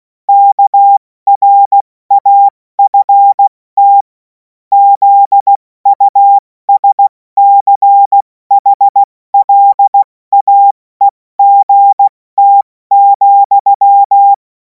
Morse code, heard as a keyed tone, about 16 wpm, 800 Hz. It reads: KRAFT ZUSCHLAEGT,
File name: Morsekurs / Witze / Witz 17. Februar 2026 70 F80